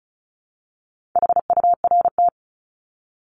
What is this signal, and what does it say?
Morse code, heard as a keyed tone, about 34 wpm, 700 Hz. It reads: HURT
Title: Top 1751-2000 Words 35wpm